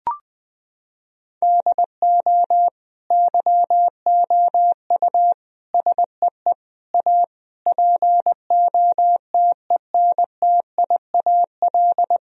Morse code, heard as a keyed tone, about 20 wpm, 700 Hz. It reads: DO YOU SEE A POTENTIAL
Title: Sentences from Top 1000 Words - Review Part 4 - 20wpm